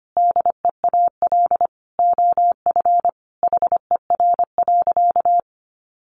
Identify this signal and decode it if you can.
Morse code, heard as a keyed tone, about 25 wpm, 700 Hz. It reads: DEAL OF HER.
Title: Sentences from Top 800 Words 25wpm